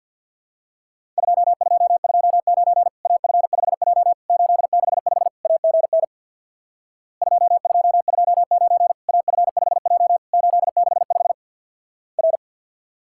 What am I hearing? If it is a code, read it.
Morse code, heard as a keyed tone, about 50 wpm, 700 Hz. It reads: 2229 AF5J 865 AGN 2229 AF5J 865 R
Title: Worked All Europe DX Contest - QTC Exchange with Repeats - 50wpm